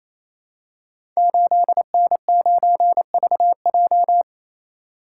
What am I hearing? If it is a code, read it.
Morse code, heard as a keyed tone, about 28 wpm, 700 Hz. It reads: 8N9VJ
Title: Code Groups - 5 Letter:Numbers - 28wpm